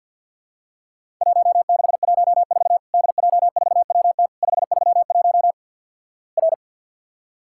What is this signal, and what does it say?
Morse code, heard as a keyed tone, about 48 wpm, 700 Hz. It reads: 1614 DJ4WT 531 R
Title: Worked All Europe DX Contest - QTC Exchange with Repeats - 50wpm